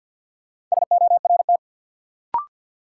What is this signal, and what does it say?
Morse code, heard as a keyed tone, about 52 wpm, 700 Hz. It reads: SORT
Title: Sets of 2 Words - New Word Drill - 50wpm